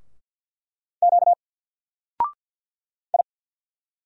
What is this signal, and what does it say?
Morse code, heard as a keyed tone, about 50 wpm, 700 Hz. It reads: Q I
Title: Single Letter-Number 50wpm